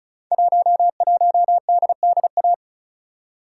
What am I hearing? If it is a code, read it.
Morse code, heard as a keyed tone, about 35 wpm, 700 Hz. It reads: 11DDA
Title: Code Groups - 5 Letter:Numbers - 35wpm